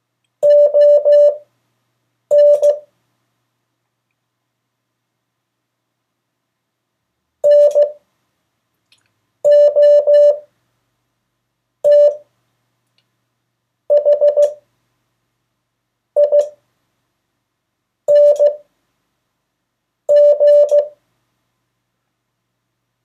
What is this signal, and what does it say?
Morse code, heard as a keyed tone, about 15 wpm, spaced as for 5 wpm, 580 Hz. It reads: ON NOTHING
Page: CW Class Review – Delaware Valley Radio Association